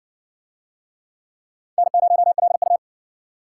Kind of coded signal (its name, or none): Morse code